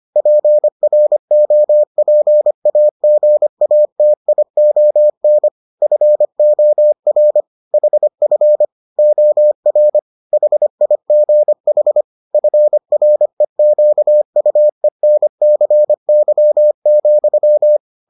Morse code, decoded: PROPAGATION FOR HF OR HIGH FREQUENCY,

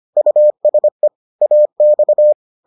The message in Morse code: USE AX

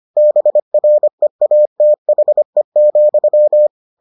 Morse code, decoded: BREATHE,

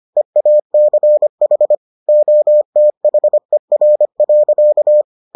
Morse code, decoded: EACH OTHER.

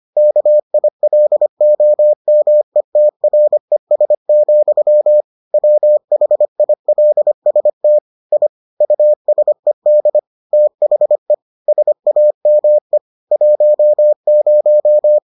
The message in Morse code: KILOMETRES, WHILST I USED THE SAME 10

